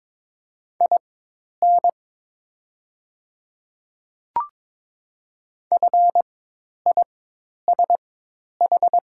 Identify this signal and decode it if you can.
Morse code, heard as a keyed tone, about 22 wpm, spaced as for 10 wpm, 700 Hz. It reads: IN FISH
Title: 4x Character Spacing - Top 300 Words - 22wpm